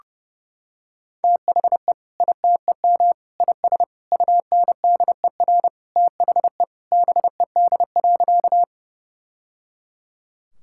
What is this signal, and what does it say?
Morse code, heard as a keyed tone, about 30 wpm, 700 Hz. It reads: THE ITEM IS UNDER THE BED.
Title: Sentences from Top 1000 Words - Review Part 6 - 30wpm